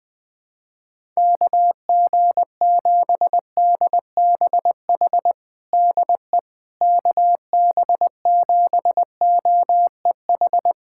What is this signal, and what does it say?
Morse code, heard as a keyed tone, about 20 wpm, 700 Hz. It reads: KG7DBH DE KB7OEH
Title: USA Callsign DE Callsign - 6 Characters - 20wpm